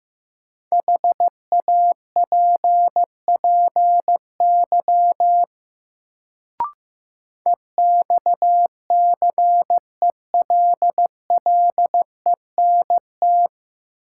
Morse code, heard as a keyed tone, about 15 wpm, 700 Hz. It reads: HAPPY EXCELLENT
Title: Sentences from Top 1000 Words - New Word Drill - 15wpm